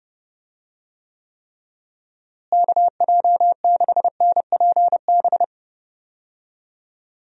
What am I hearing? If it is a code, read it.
Morse code, heard as a keyed tone, about 30 wpm, 700 Hz. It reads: KJ6NPB
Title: USA Call Signs - 6 Characters - Repeated 3x - 30wpm